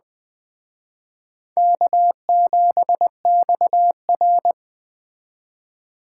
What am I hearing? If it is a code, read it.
Morse code, heard as a keyed tone, about 20 wpm, 700 Hz. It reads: K7XR